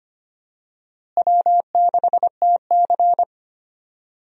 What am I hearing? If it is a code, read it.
Morse code, heard as a keyed tone, about 25 wpm, 700 Hz. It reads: W6TC
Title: USA Call Signs - 4 Characters - Repeated 2x - 25wpm